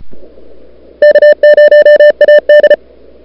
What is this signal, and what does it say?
Morse code, heard as a keyed tone, about 34 wpm, 590 Hz. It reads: K0AD
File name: n6tr / Fab2007CWSprintHour1